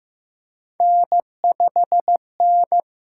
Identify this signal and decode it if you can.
Morse code, heard as a keyed tone, about 15 wpm, 700 Hz. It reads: N5N
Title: USA Call Signs - 5 Characters - 8x Character Spacing - 15wpm